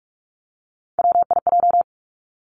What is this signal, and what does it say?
Morse code, heard as a keyed tone, about 46 wpm, 700 Hz. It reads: WIJ
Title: Three Challenging Letter-Number Combinations 45wpm